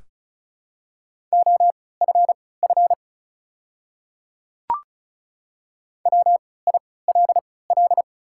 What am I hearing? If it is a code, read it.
Morse code, heard as a keyed tone, about 36 wpm, spaced as for 20 wpm, 690 Hz. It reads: OFF WILL
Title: 3x Character Spacing - Top 300 Words - 35wpm